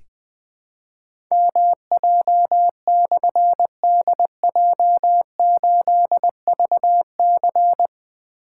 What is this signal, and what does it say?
Morse code, heard as a keyed tone, about 20 wpm, 700 Hz. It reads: MJ/DJ8VC
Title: Difficult Call Signs 20wpm